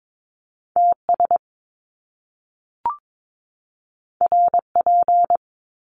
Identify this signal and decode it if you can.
Morse code, heard as a keyed tone, about 22 wpm, 700 Hz. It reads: TS RP
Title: Top Two Letter Combinations in Words 22wpm